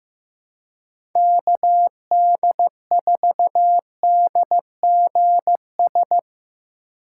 Morse code, heard as a keyed tone, about 15 wpm, 700 Hz. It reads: KD4DGS
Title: USA Call Signs - 6 Characters - 4x Character Spacing - 15wpm